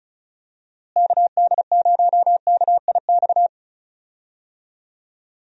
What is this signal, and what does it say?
Morse code, heard as a keyed tone, about 36 wpm, 700 Hz. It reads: KD0KIX
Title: USA Call Signs - 6 Characters - Repeated 3x - 35wpm